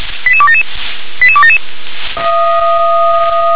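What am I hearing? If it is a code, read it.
Five-tone selective call (ZVEI), heard as quick runs of five tones, tones of about 70 ms. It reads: 88277 88277
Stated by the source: ZVEI five-tone selective call 88007, received by SDR.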